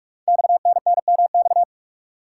Morse code, heard as a keyed tone, about 44 wpm, 700 Hz. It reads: XNNMX
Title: Code Groups - 5 Letters - 45wpm